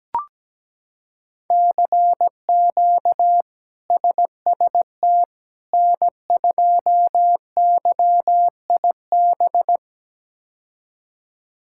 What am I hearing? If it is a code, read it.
Morse code, heard as a keyed tone, about 17 wpm, 700 Hz. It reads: CQ SST N2YIB
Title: K1USN SST QSO Practice - 17wpm